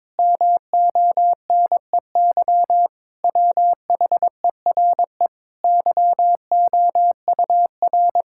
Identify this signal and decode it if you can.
Morse code, heard as a keyed tone, about 22 wpm, 700 Hz. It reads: MONEY WHERE YOUR
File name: Sentences with 7-Word Phrases - 22wpm